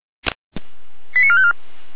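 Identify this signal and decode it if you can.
Five-tone selective call (ZVEI), heard as quick runs of five tones, tones of about 70 ms. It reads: 89454
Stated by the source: ZVEI five-tone call received on SDR.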